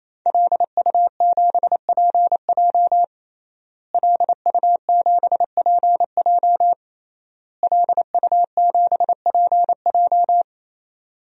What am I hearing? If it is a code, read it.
Morse code, heard as a keyed tone, about 28 wpm, 700 Hz. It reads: LU7PJ LU7PJ LU7PJ
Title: DXCC-100 Call Signs - 5 Characters - Repeated 3x - 28wpm